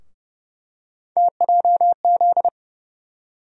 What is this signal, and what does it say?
Morse code, heard as a keyed tone, about 30 wpm, 700 Hz. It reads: TJZ
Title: USA Call Sign Postfixes - 3 Letters 30wpm